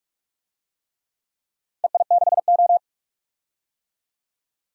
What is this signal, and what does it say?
Morse code, heard as a keyed tone, about 43 wpm, 700 Hz. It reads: EI6O
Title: DXCC-100 Call Signs - 4 Characters - 45wpm